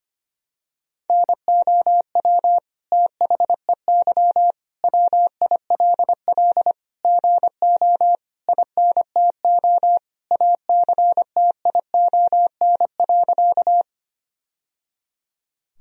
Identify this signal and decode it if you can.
Morse code, heard as a keyed tone, about 25 wpm, 700 Hz. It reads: NOW THEY WILL GO INTO ACTION.
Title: Sentences from Top 1000 Words - Review Part 4 - 25wpm